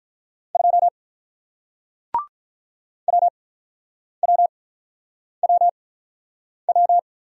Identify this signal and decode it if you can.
Morse code, heard as a keyed tone, about 48 wpm, spaced as for 9 wpm, 700 Hz. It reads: 2 WWWW